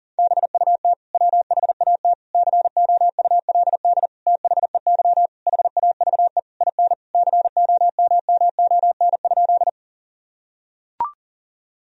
Morse code, heard as a keyed tone, about 37 wpm, 700 Hz. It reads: BUT WHAT COULD THEY HAVE IN COMMON?
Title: Sentences from Top 1000 Words - Review Part 3 - 40wpm